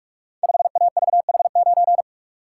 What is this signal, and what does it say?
Morse code, heard as a keyed tone, about 46 wpm, 700 Hz. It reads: 5AVH9
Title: Code Groups - 5 Letter:Numbers - 45wpm